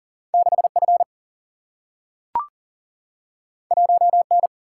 Morse code, heard as a keyed tone, about 42 wpm, 700 Hz. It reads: 6F 1N
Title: Single-Letter Single-Number 40wpm